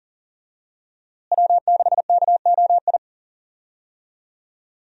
Morse code, heard as a keyed tone, about 37 wpm, 700 Hz. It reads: W6KOI
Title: USA Call Signs - 5 Characters - Repeated 3x - 40wpm